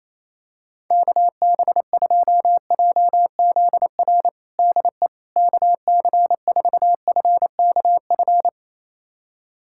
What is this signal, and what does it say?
Morse code, heard as a keyed tone, about 27 wpm, 700 Hz. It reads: KB2JZR DE KC4FKF
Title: USA Callsign DE Callsign - 6 Characters - 28wpm